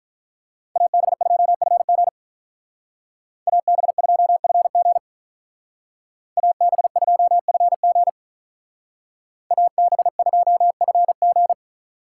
Morse code, heard as a keyed tone, about 43 wpm, 700 Hz. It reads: AB2FG AB2FG AB2FG AB2FG
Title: Speed Racing - USA 2x2 Call Signs 35wpm